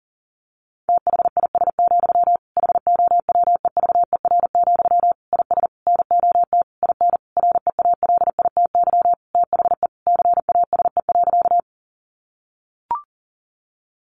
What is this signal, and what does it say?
Morse code, heard as a keyed tone, about 40 wpm, 700 Hz. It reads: THIS, HOWEVER, IS NOT IN REALITY THE CASE.
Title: Sentences from Top 1200 Words 40wpm